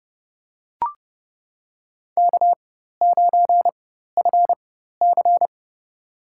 Morse code, heard as a keyed tone, about 32 wpm, spaced as for 14 wpm, 700 Hz. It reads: K9FC
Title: USA Call Signs - 4 Characters - 4x Character Spacing - 30wpm